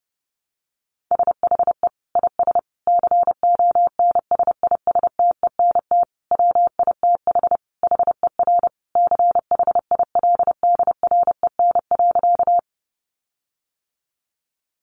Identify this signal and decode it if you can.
Morse code, heard as a keyed tone, about 30 wpm, 700 Hz. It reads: SHE IS CONSISTENT WITH HER CHILDREN.
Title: Sentences from Top 1500 Words 30wpm